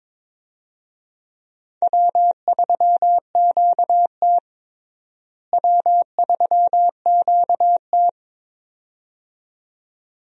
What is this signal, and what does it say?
Morse code, heard as a keyed tone, about 21 wpm, 700 Hz. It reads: W3QT W3QT
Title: USA Call Signs - 4 Characters - Repeated 2x - 22wpm